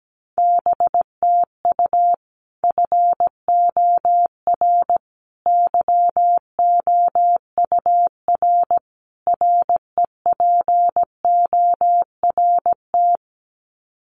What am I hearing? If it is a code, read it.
Morse code, heard as a keyed tone, about 17 wpm, 700 Hz. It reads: BTU FOR YOUR REPORT
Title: Common QSO Phrases 17wpm